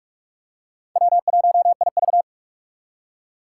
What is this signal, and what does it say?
Morse code, heard as a keyed tone, about 46 wpm, 700 Hz. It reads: W1IV
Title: Four Challenging Letter-Number Combinations 45wpm